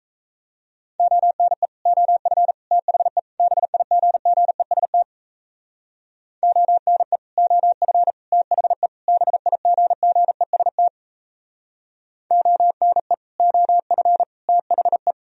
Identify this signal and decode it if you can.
Morse code, heard as a keyed tone, about 34 wpm, 700 Hz. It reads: ONE OF THE BIGGEST ONE OF THE BIGGEST ONE OF THE
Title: Speed-Racing - Sets of 4 Words 28wpm